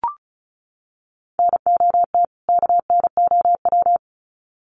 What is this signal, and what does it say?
Morse code, heard as a keyed tone, about 34 wpm, 700 Hz. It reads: NOT KNOW
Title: Sets of 2 Words 35wpm